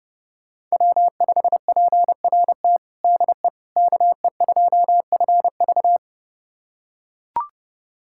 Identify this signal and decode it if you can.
Morse code, heard as a keyed tone, about 30 wpm, 700 Hz. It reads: W5PRT DE KE2FV
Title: USA Callsign DE Callsign - 5 Characters - 30wpm